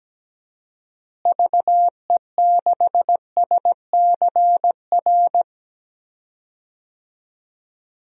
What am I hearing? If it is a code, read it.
Morse code, heard as a keyed tone, about 17 wpm, 700 Hz. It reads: VE6SCR
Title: DXCC-100 Call Signs - 6 Characters - Repeated 3x - 17wpm